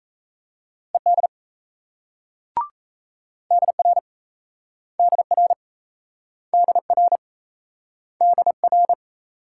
Morse code, heard as a keyed tone, about 38 wpm, 700 Hz. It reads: ED DR DR DR DR